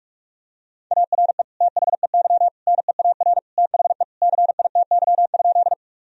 Morse code, heard as a keyed tone, about 43 wpm, 700 Hz. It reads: ARE THEY NEAR THE CITY?
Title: Sentences from Top 1000 Words - Review Part 1 - 45wpm